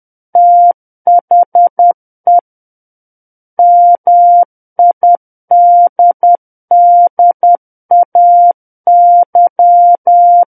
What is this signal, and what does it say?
Morse code, heard as a keyed tone, about 10 wpm, 700 Hz. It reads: THE MIDDAY
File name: Book One / 01-Chapter 1 - 10 WPM